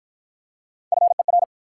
Morse code, heard as a keyed tone, about 51 wpm, 700 Hz. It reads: FER